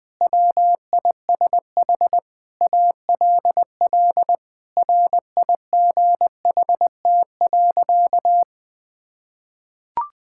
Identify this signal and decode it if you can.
Morse code, heard as a keyed tone, about 20 wpm, 700 Hz. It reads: WISH ALL RIGHT.
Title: Sentences from Top 1600 Words 20wpm